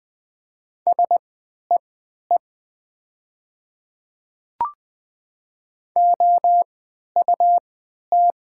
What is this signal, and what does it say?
Morse code, heard as a keyed tone, about 20 wpm, spaced as for 11 wpm, 700 Hz. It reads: SEE OUT
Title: 3x Character Spacing - Top 300 Words - 20wpm